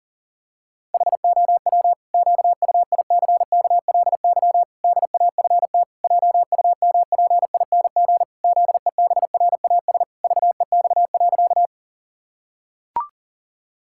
Morse code, heard as a keyed tone, about 40 wpm, 700 Hz. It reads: HOW QUICKLY DAFT JUMPING ZEBRAS VEX.
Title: Pangram Challenge v2 - 40wpm